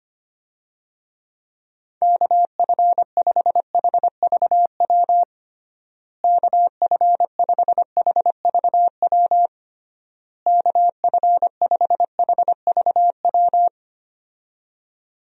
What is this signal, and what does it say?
Morse code, heard as a keyed tone, about 25 wpm, 700 Hz. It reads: KF5HVW KF5HVW KF5HVW
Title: USA Call Signs - 6 Characters - Repeated 3x - 25wpm